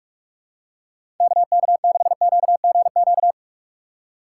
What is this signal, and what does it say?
Morse code, heard as a keyed tone, about 46 wpm, 700 Hz. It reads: KK6QGQ
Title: USA Call Signs - 6 Characters - 2x Character Spacing - 45wpm